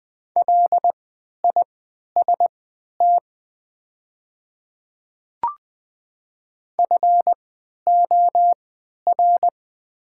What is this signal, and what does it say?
Morse code, heard as a keyed tone, about 20 wpm, spaced as for 11 wpm, 700 Hz. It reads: LIST FOR
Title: 3x Character Spacing - Top 500 Words - 20wpm